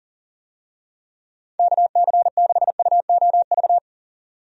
Morse code, heard as a keyed tone, about 40 wpm, 700 Hz. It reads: KC6UOV
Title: USA Call Signs - 6 Characters - 40wpm